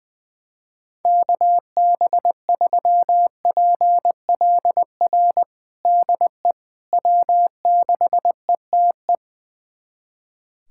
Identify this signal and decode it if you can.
Morse code, heard as a keyed tone, about 20 wpm, 700 Hz. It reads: KB3PLR DE W6ETE
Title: USA Callsign DE Callsign 20wpm